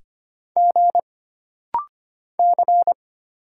Morse code, heard as a keyed tone, about 25 wpm, 700 Hz. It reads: G C
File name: Single Character (Letter:Number:Punctuation:Prosign) - Rapid-Fire - 25wpm